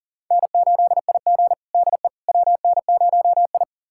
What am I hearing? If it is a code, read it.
Morse code, heard as a keyed tone, about 40 wpm, 690 Hz. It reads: N8IG DE WN0I
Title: USA Callsign DE Callsign - 4 Characters - 40wpm